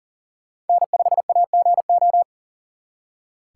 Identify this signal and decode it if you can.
Morse code, heard as a keyed tone, about 51 wpm, 700 Hz. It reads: N5AGO